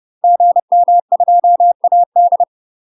Morse code, heard as a keyed tone, about 30 wpm, 700 Hz. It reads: GM2AD
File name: UK MASTER QSOs TRAINING / 30WPM / UK MASTER QSOs 0000